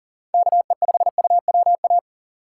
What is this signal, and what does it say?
Morse code, heard as a keyed tone, about 40 wpm, 700 Hz. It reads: KE5UWA
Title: USA Call Signs - 6 Characters - 6x Character Spacing - 40wpm